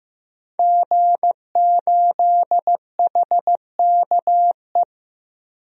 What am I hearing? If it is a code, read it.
Morse code, heard as a keyed tone, about 15 wpm, 700 Hz. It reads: G8HKE
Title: DXCC-100 Call Signs - 5 Characters - Repeated 3x - 15wpm